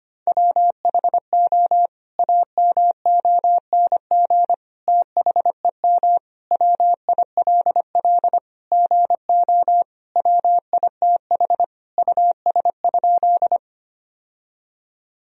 Morse code, heard as a keyed tone, about 25 wpm, 700 Hz. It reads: WHO AMONG THEM WILL GO WITH US?
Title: Sentences from Top 1000 Words - Review Part 3 - 25wpm